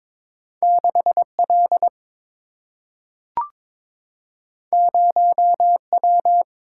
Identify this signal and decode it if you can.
Morse code, heard as a keyed tone, about 22 wpm, 690 Hz. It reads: 6L 0W
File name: practice / 22wpm / Two Letter-Number Combinations - 22wpm